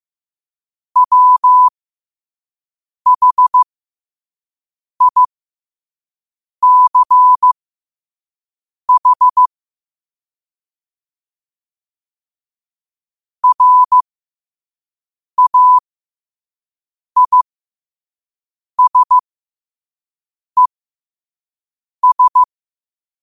Morse code, heard as a keyed tone, about 15 wpm, spaced as for 5 wpm, 1000 Hz. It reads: WHICH RAISES
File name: qotdfiles / 5wpm